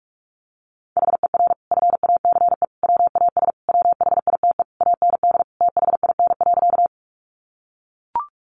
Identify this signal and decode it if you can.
Morse code, heard as a keyed tone, about 44 wpm, 700 Hz. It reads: HER FACE WAS WHITE AND THIN.
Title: Sentences from Top 1200 Words 45wpm